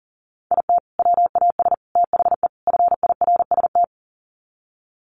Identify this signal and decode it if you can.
Morse code, heard as a keyed tone, about 37 wpm, 700 Hz. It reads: IT WAS THE FIRST
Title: Sets of 4 Words 40wpm